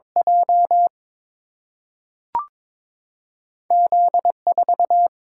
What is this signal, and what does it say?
Morse code, heard as a keyed tone, about 22 wpm, 700 Hz. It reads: J Z 4
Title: Single-Letter Single-Number 22wpm